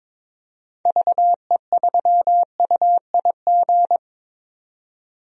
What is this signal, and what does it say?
Morse code, heard as a keyed tone, about 22 wpm, 700 Hz. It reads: VE3UIG
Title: DXCC-100 Call Signs - 6 Characters - 22wpm